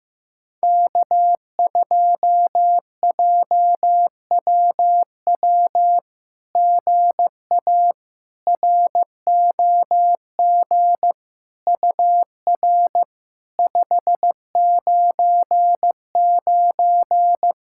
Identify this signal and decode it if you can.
Morse code, heard as a keyed tone, about 15 wpm, 700 Hz. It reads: K2JWW GA ROG UR 599
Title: SOTA QSO Practice 15wpm